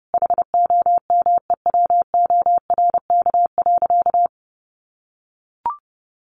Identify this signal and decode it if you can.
Morse code, heard as a keyed tone, about 30 wpm, 700 Hz. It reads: HOMEWORK.
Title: Additional Easy Phrases 30wpm